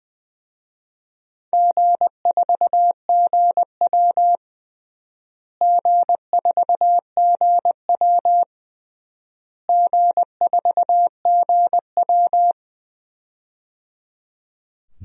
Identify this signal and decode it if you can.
Morse code, heard as a keyed tone, about 20 wpm, 700 Hz. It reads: G4GW G4GW G4GW
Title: DXCC-100 Call Signs - 4 Characters - Repeated 3x - 20wpm